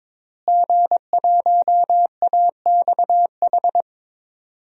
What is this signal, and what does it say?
Morse code, heard as a keyed tone, about 22 wpm, 700 Hz. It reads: G1AXH
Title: DXCC-100 Call Signs - 5 Characters - Repeated 2x - 22wpm